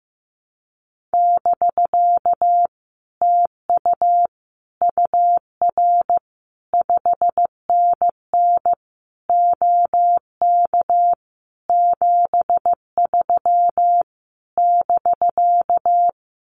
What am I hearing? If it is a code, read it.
Morse code, heard as a keyed tone, about 15 wpm, 700 Hz. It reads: <BK> TU UR 5NN OK 73 <BK>
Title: Common QSO Phrases Encore 15wpm